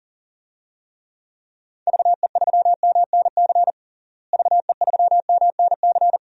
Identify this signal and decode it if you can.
Morse code, heard as a keyed tone, about 40 wpm, 700 Hz. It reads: VE3MNC VE3MNC
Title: DXCC-100 Call Signs - 6 Characters - Repeated 3x - 40wpm